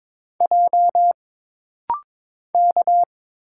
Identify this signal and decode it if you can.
Morse code, heard as a keyed tone, about 22 wpm, 700 Hz. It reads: J K
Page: This rapid-fire timing is geared to push your ICR (Instant Character Recognition) skills.